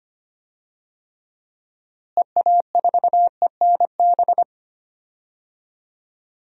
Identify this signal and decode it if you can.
Morse code, heard as a keyed tone, about 25 wpm, 700 Hz. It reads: EA4ENB
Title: DXCC-100 Call Signs - 6 Characters - Repeated 3x - 25wpm